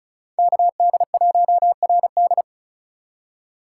Morse code, heard as a keyed tone, about 43 wpm, 700 Hz. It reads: KD1RD
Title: Speed Racing - USA 2x2 Call Signs 35wpm